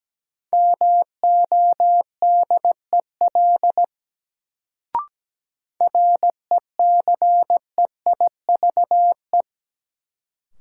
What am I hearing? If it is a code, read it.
Morse code, heard as a keyed tone, about 17 wpm, 700 Hz. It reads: MODEL RECEIVE
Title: Sentences from Top 1000 Words - New Word Drill - 17wpm